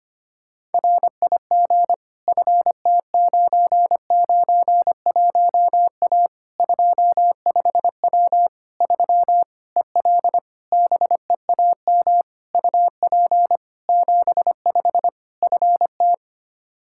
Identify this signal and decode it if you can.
Morse code, heard as a keyed tone, about 25 wpm, 700 Hz. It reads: RIG FT991A 25W 3 EL BEAM UP 75 FT
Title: K1USN SST QSO Practice 25wpm